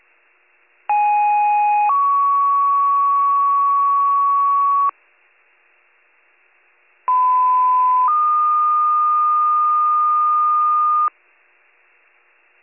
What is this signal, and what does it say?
Two-tone page, heard as a one-second tone, then a three-second tone, about 830 then 1120 Hz, about 1010 then 1220 Hz. It is heard from 0.9 to 11.1 s.